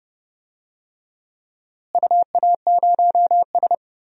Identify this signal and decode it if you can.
Morse code, heard as a keyed tone, about 30 wpm, 700 Hz. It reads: UA0S
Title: DXCC-100 Call Signs - 4 Characters - Repeated 3x - 30wpm